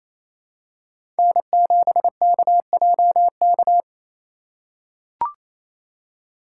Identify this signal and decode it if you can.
Morse code, heard as a keyed tone, about 28 wpm, 700 Hz. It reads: N7KJK
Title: USA Call Signs - 5 Characters - 2x Character Spacing - 28wpm